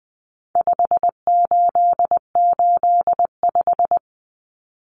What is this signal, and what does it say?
Morse code, heard as a keyed tone, about 20 wpm, 700 Hz. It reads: 5885
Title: Numbers - 4-Digit 20wpm